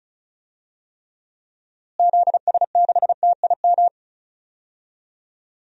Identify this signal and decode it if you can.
Morse code, heard as a keyed tone, about 36 wpm, 700 Hz. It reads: ZS6TIM
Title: DXCC-100 Call Signs - 6 Characters - Repeated 2x - 35wpm